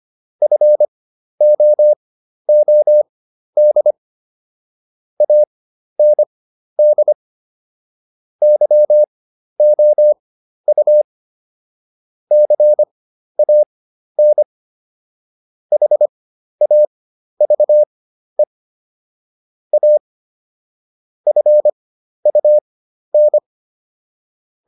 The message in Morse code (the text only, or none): FOOD AND YOU CAN HAVE A FUN